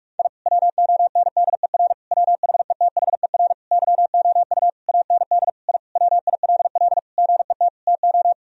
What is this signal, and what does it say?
Morse code, heard as a keyed tone, about 42 wpm, 700 Hz. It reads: I WONDER WHETHER YOU AND I WILL GET TO
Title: Sentences from Top 1000 Words - Review Part 3 - 45wpm